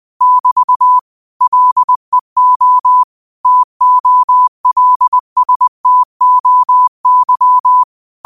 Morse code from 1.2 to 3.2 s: LEO

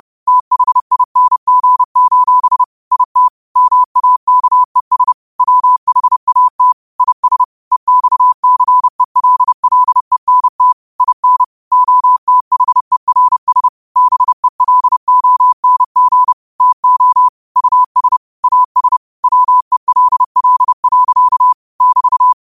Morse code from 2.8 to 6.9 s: IT MAKES WHAT

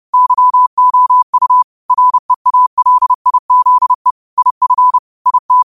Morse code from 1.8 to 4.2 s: REALIZE

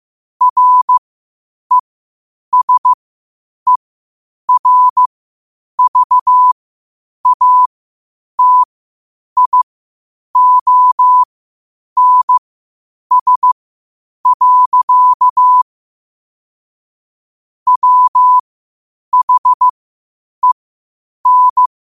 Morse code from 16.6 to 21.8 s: WHEN

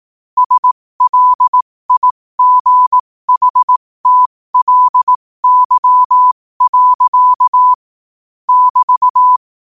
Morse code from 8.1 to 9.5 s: <BT>

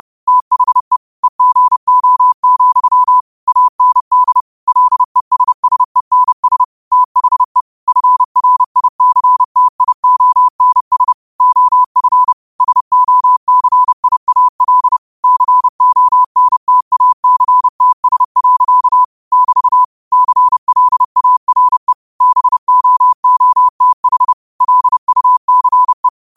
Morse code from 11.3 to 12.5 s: OF

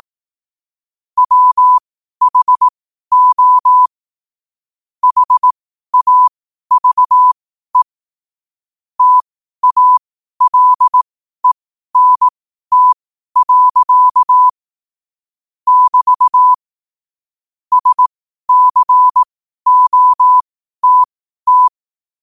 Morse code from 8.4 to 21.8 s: TALENT. <BT> SCOTT

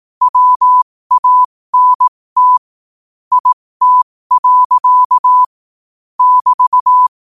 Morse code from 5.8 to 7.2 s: <BT>

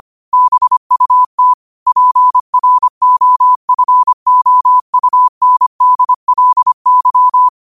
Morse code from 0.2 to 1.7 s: BUT

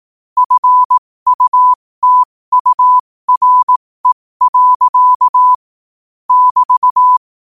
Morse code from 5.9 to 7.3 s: <BT>

